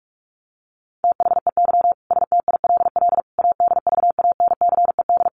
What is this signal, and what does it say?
Morse code, heard as a keyed tone, about 44 wpm, 700 Hz. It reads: THEY STILL ADVANCED
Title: Sentences from Top 1200 Words 45wpm